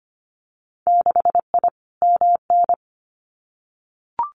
Morse code, heard as a keyed tone, about 25 wpm, 700 Hz. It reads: 6I MN
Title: Winter Field Day Exchanges Encore 25wpm